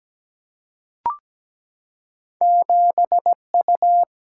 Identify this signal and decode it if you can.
Morse code, heard as a keyed tone, about 17 wpm, 700 Hz. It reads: 7U
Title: Single-Letter Single-Number 17wpm